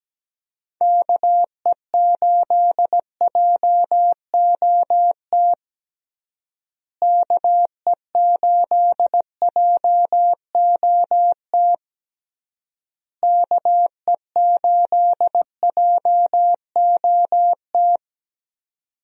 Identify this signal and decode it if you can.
Morse code, heard as a keyed tone, about 17 wpm, 700 Hz. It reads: KE8JOT KE8JOT KE8JOT
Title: USA Call Signs - 6 Characters - Repeated 3x - 17wpm